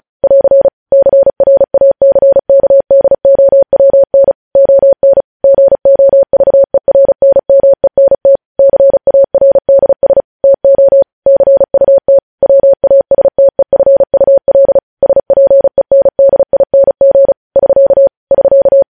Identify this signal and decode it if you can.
Morse code, heard as a keyed tone, about 36 wpm, 550 Hz. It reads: <AR> CRACKDOWN ON GOVERNMENT CARDS TO CUT WASTEFUL SPENDING <SK> <SK>
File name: podcast / News headlines-1x-35x35